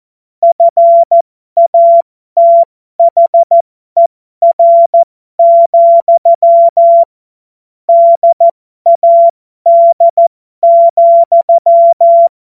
Morse code, heard as a keyed tone, about 14 wpm, spaced as for 12 wpm, 680 Hz. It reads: FATHER, DAD,
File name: Morsekurs / Witze